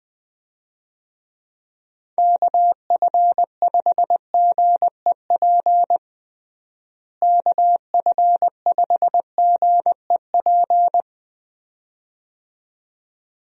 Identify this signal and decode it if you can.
Morse code, heard as a keyed tone, about 21 wpm, 700 Hz. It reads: KF5GEP KF5GEP
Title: USA Call Signs - 6 Characters - Repeated 2x - 20wpm